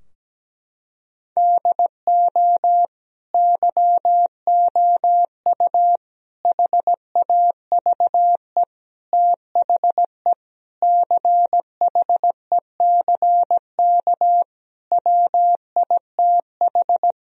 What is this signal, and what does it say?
Morse code, heard as a keyed tone, about 17 wpm, 700 Hz. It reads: DO YOU HAVE THE CHECK WITH
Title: Sentences from Top 1000 Words - Review Part 3 - 17wpm